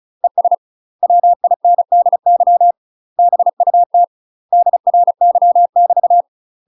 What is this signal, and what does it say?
Morse code, heard as a keyed tone, about 35 wpm, 700 Hz. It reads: ES WINDY BUT DRY <BT>
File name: UK MASTER QSOs TRAINING/35WPM